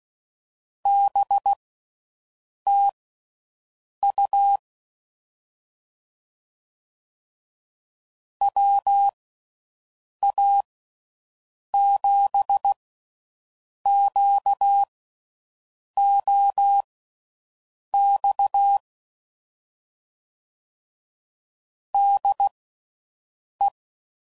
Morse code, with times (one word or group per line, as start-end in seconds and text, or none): 0.9-4.6 BTU
8.4-18.8 WA7QOX
21.9-23.7 DE